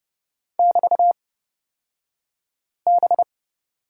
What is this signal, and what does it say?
Morse code, heard as a keyed tone, about 31 wpm, spaced as for 5 wpm, 690 Hz. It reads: <BT>B
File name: Lesson 36d - BT - Two Characters 30wpm